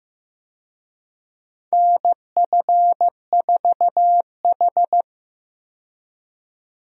Morse code, heard as a keyed tone, about 15 wpm, 700 Hz. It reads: NF4H
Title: USA Call Signs - 4 Characters - 15wpm